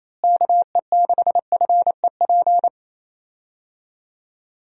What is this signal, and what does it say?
Morse code, heard as a keyed tone, about 28 wpm, 700 Hz. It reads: KE6FEP
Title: USA Call Signs - 6 Characters - Repeated 2x - 28wpm